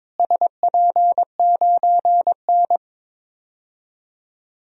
Morse code, heard as a keyed tone, about 22 wpm, 700 Hz. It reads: SP9N
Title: DXCC-100 Call Signs - 4 Characters - Repeated 2x - 22wpm